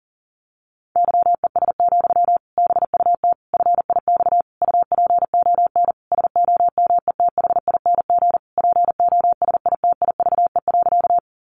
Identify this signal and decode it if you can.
Morse code, heard as a keyed tone, about 40 wpm, 700 Hz. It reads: YES, BUT FIX UPON SOMETHING POSITIVE.
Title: Sentences from Top 1400 Words 40wpm